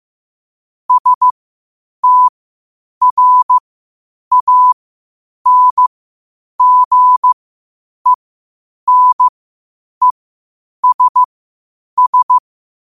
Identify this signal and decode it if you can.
Morse code, heard as a keyed tone, about 15 wpm, spaced as for 8 wpm, 1000 Hz. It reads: STRANGENESS
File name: qotdfiles / 8wpm